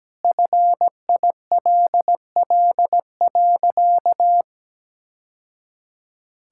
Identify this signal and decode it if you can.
Morse code, heard as a keyed tone, about 17 wpm, 700 Hz. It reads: FILL.
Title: Sentences from Top 1000 Words - Review Part 5 - 17wpm